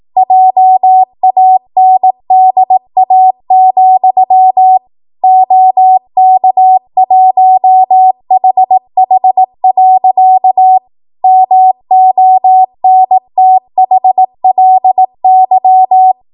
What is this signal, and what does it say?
Morse code, heard as a keyed tone, about 18 wpm, 750 Hz. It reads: JANDA, OK1HH. MONTHLY
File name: Archive / 18 WPM / 18WPM